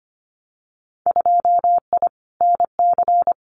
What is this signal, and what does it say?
Morse code, heard as a keyed tone, about 25 wpm, 700 Hz. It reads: 2I NC
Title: Winter Field Day Exchanges 25wpm